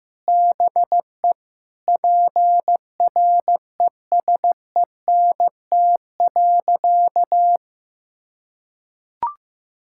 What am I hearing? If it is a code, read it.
Morse code, heard as a keyed tone, about 16 wpm, 700 Hz. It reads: BE PRESENT.
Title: Sentences from Top 1600 Words 15wpm